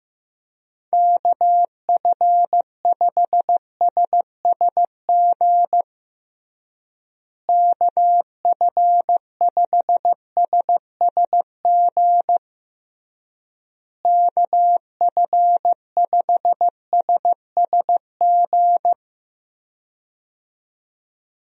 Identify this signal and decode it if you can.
Morse code, heard as a keyed tone, about 15 wpm, 700 Hz. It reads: KF5SSG KF5SSG KF5SSG
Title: USA Call Signs - 6 Characters - Repeated 3x - 15wpm